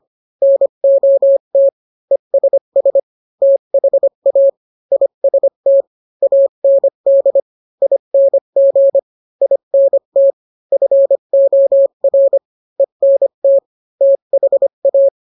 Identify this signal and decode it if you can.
Morse code, heard as a keyed tone, about 25 wpm, 550 Hz. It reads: NOT ESS THA IST AND ING INT FOR ENT THA